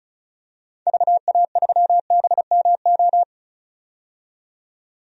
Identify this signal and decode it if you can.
Morse code, heard as a keyed tone, about 36 wpm, 700 Hz. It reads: VA3BMO
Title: DXCC-100 Call Signs - 6 Characters - 35wpm